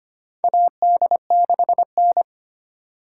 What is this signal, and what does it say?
Morse code, heard as a keyed tone, about 25 wpm, 700 Hz. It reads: AD6N
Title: USA Call Signs - 4 Characters - 2x Character Spacing - 25wpm